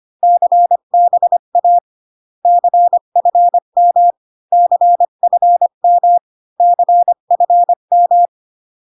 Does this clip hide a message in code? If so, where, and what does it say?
Yes: Morse code from 0.2 to 8.3 s, heard as a keyed tone, about 25 wpm, 700 Hz. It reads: CBA CFM CFM CFM